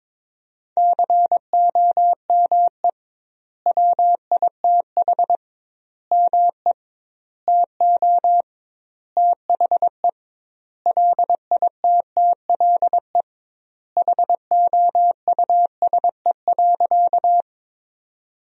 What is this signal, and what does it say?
Morse code, heard as a keyed tone, about 22 wpm, 700 Hz. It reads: COME WITH ME TO THE LITTLE HOUSE.
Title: 2x Word-Spacing - Sentences from Top 200 Words - 22wpm